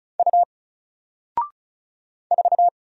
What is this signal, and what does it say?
Morse code, heard as a keyed tone, about 36 wpm, 700 Hz. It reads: U 4
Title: Single Letter-Number Alt 35wpm